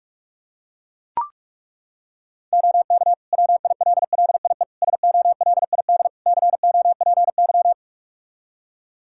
Morse code, heard as a keyed tone, about 42 wpm, 700 Hz. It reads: OK WILLIE SOLID COPY
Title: Practice QSO - Exchange 3 - More Info - 45wpm